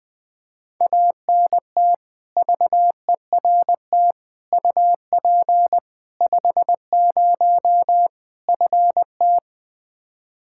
Morse code, heard as a keyed tone, about 20 wpm, 700 Hz. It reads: ANT VERT UP 50 FT
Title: Practice QSO - Exchange 2 - Info - 20wpm